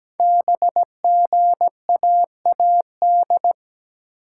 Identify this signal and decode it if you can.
Morse code, heard as a keyed tone, about 17 wpm, 700 Hz. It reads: BGAAD
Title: Code Groups - 5 Letters - 17wpm